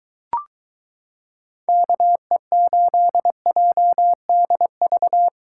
Morse code, heard as a keyed tone, about 23 wpm, 700 Hz. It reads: KE8JDV